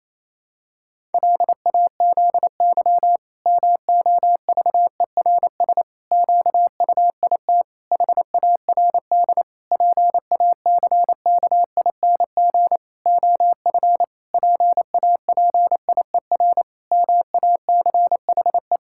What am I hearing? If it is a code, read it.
Morse code, heard as a keyed tone, about 28 wpm, 700 Hz. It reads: LAZY MOVERS QUIT HARD PACKING OF PAPIER MACHE
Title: Pangram Challenge v2 - 28wpm